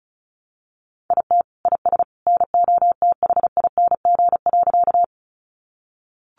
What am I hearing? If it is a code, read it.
Morse code, heard as a keyed tone, about 34 wpm, 700 Hz. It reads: IT IS NOTHING.
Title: Sentences from Top 500 Words 35wpm